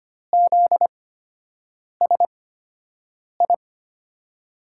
Morse code, heard as a keyed tone, about 25 wpm, spaced as for 7 wpm, 700 Hz. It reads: ZSI
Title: USA Call Signs - 6 Characters - 8x Character Spacing - 25wpm